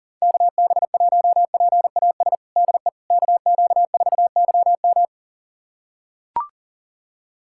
Morse code, heard as a keyed tone, about 37 wpm, 700 Hz. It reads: KB1PAS DE KQ4YM
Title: USA Callsign DE Callsign 40wpm